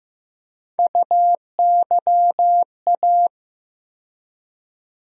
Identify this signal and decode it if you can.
Morse code, heard as a keyed tone, about 15 wpm, 700 Hz. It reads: UYA
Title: USA Call Sign Postfixes - 3 Letters 15wpm